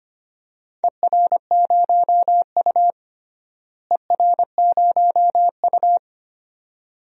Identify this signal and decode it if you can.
Morse code, heard as a keyed tone, about 25 wpm, 700 Hz. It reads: ER0U ER0U
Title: DXCC-100 Call Signs - 4 Characters - Repeated 2x - 25wpm